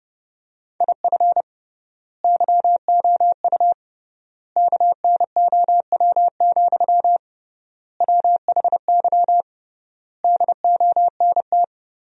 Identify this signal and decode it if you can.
Morse code, heard as a keyed tone, about 30 wpm, 700 Hz. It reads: IF YOU KNOW, WHY DONT
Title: 3x Word-Spacing - Sentences from Top 300 Words - 30wpm